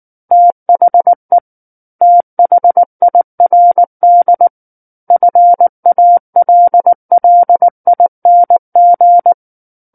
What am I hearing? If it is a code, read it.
Morse code, heard as a keyed tone, about 19 wpm, 700 Hz. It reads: THE THIRD FALLING
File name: Book One / 10-Chapter 10 - 19 WPM